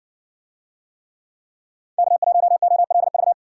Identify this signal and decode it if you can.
Morse code, heard as a keyed tone, about 57 wpm, 700 Hz. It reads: K1OLV